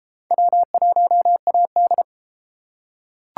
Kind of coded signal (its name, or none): Morse code